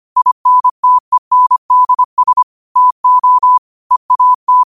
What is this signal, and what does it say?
Morse code, heard as a keyed tone, about 25 wpm, 1000 Hz. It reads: INTENDS TO EAT
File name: qotdfiles / 25wpm